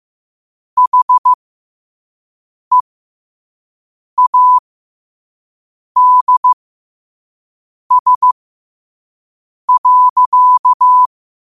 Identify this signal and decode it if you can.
Morse code, heard as a keyed tone, about 15 wpm, spaced as for 5 wpm, 1000 Hz. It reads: HEADS.
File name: qotdfiles / 5wpm